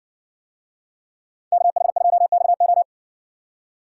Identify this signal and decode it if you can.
Morse code, heard as a keyed tone, about 57 wpm, 700 Hz. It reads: KH2XO